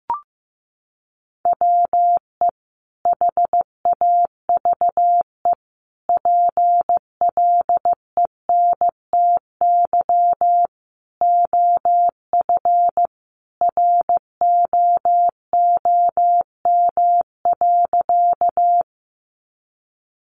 Morse code, heard as a keyed tone, about 15 wpm, 700 Hz. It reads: WE HAVE PLENTY OF ROOM.
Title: Sentences from Top 1300 Words 15wpm